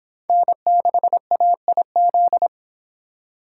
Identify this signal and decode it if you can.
Morse code, heard as a keyed tone, about 25 wpm, 700 Hz. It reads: N6AIZ